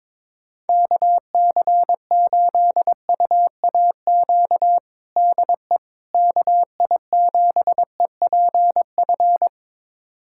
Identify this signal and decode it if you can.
Morse code, heard as a keyed tone, about 22 wpm, 700 Hz. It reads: KC8UAQ DE KI7EPF
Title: USA Callsign DE Callsign - 6 Characters - 22wpm